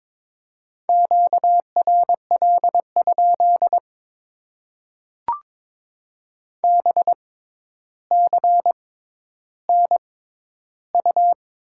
Morse code, heard as a keyed tone, about 22 wpm, spaced as for 11 wpm, 700 Hz. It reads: QRL? BCNU
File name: Top 90 QSO Elements - 6x Character Spacing - 22wpm